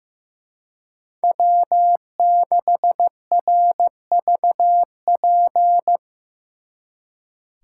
Morse code, heard as a keyed tone, about 18 wpm, 700 Hz. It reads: W6RVP
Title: Speed Racing - USA Call Signs 15wpm